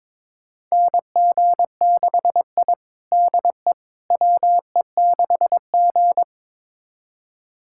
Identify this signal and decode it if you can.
Morse code, heard as a keyed tone, about 22 wpm, 700 Hz. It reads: NG6I DE WE6G
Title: USA Callsign DE Callsign - 4 Characters - 22wpm